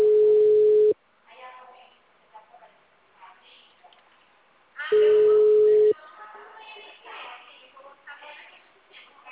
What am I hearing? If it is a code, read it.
Ringback tone (Europe, 425 Hz), with speech.